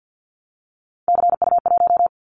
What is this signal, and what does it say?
Morse code, heard as a keyed tone, about 50 wpm, 700 Hz. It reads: CU1
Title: Callsign Prefixes for Top 100 DXCC - 50wpm